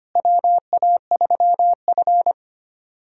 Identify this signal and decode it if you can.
Morse code, heard as a keyed tone, about 25 wpm, 700 Hz. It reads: WA3F
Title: USA Call Signs - 4 Characters - Repeated 3x - 25wpm